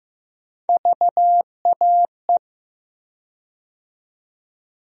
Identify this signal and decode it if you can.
Morse code, heard as a keyed tone, about 15 wpm, 700 Hz. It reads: VAE
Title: USA Call Sign Postfixes - 3 Letters 15wpm